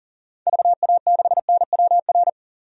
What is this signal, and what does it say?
Morse code, heard as a keyed tone, about 40 wpm, 700 Hz. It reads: VA6NWR
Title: DXCC-100 Call Signs - 6 Characters - 40wpm